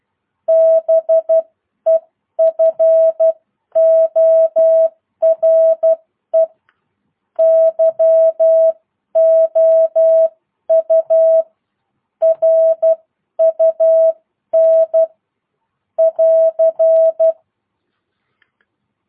Morse code, heard as a keyed tone, about 12 wpm, spaced as for 10 wpm, 650 Hz. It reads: BEFORE YOU RUN <AR>